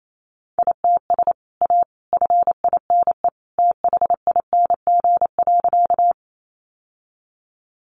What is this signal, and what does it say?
Morse code, heard as a keyed tone, about 28 wpm, 700 Hz. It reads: ITS A FINE THING.
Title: Sentences from Top 600 Words 28wpm